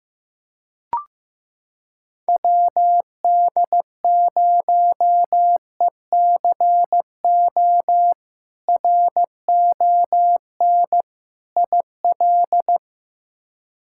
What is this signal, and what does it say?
Morse code, heard as a keyed tone, about 15 wpm, 700 Hz. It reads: WD0ECO RON IL
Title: K1USN SST QSO Practice - 15wpm